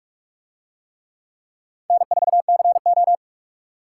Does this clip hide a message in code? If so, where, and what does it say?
Yes: Morse code from 1.9 to 3.1 s, heard as a keyed tone, about 45 wpm, 700 Hz. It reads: N4CO